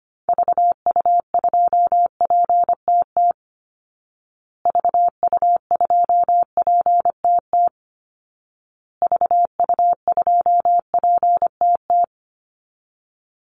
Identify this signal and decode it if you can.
Morse code, heard as a keyed tone, about 25 wpm, 700 Hz. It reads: VU2PTT VU2PTT VU2PTT
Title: Call Signs Repeated 2x - 25wpm